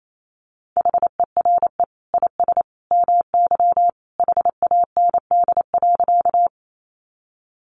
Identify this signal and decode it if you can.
Morse code, heard as a keyed tone, about 28 wpm, 700 Hz. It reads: HERE IS MY HAND.
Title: Sentences from Top 200 Words 28wpm